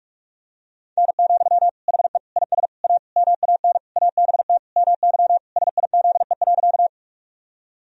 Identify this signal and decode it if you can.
Morse code, heard as a keyed tone, about 46 wpm, 700 Hz. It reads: N, HE IS A MAN ABT MY SIZE.
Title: Sentences from Top 400 Words and Abbreviations 45wpm